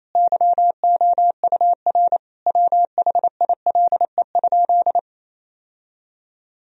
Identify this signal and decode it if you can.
Morse code, heard as a keyed tone, about 28 wpm, 700 Hz. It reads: YOUR WHILE?
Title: Sentences from Top 1000 Words - Review Part 1 - 28wpm